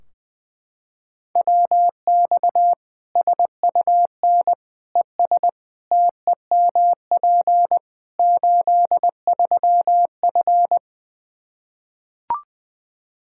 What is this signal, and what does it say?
Morse code, heard as a keyed tone, about 20 wpm, 700 Hz. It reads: WX SUN ES TEMP 83F
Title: K1USN SST QSO Practice 20wpm